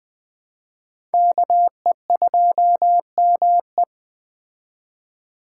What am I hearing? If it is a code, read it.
Morse code, heard as a keyed tone, about 20 wpm, 700 Hz. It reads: KE2ME
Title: Speed Racing - USA 2x2 Call Signs 17wpm